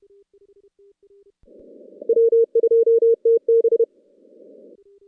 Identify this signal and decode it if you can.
Morse code, heard as a keyed tone, about 30 wpm, 440 Hz. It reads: W2TB
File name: contests / CW-SS-k5tr- -2100